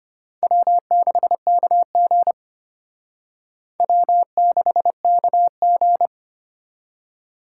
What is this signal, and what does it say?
Morse code, heard as a keyed tone, about 29 wpm, 700 Hz. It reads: W6KG W6KG
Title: Speed Racing - USA 1x2 Call Signs 25wpm